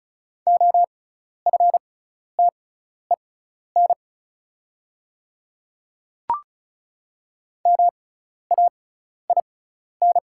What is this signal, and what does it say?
Morse code, heard as a keyed tone, about 36 wpm, spaced as for 12 wpm, 690 Hz. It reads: OFTEN MAIN
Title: 6x Character Spacing - Top 300 Words - 35wpm